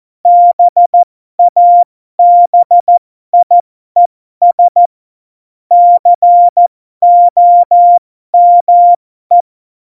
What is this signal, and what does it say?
Morse code, heard as a keyed tone, about 14 wpm, spaced as for 12 wpm, 700 Hz. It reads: BABIES COME
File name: Morsekurs / Witze